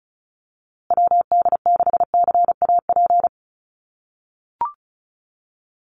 Morse code, heard as a keyed tone, about 34 wpm, 700 Hz. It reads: WD6CAP
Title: USA Call Signs Repeated 2x 35wpm